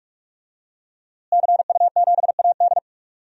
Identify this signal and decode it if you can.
Morse code, heard as a keyed tone, about 46 wpm, 700 Hz. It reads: CU7AD
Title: DXCC-100 Call Signs - 5 Characters - Repeated 3x - 45wpm